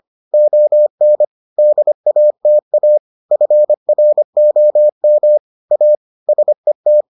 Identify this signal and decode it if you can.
Morse code, heard as a keyed tone, about 25 wpm, 600 Hz. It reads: ON DATA FROM A SET